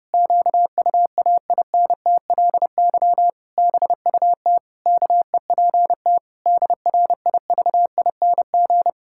Morse code, heard as a keyed tone, about 30 wpm, 700 Hz. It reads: QUAINTLY BUT KEPT DRIVING
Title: Pangram Challenge v2 - 30wpm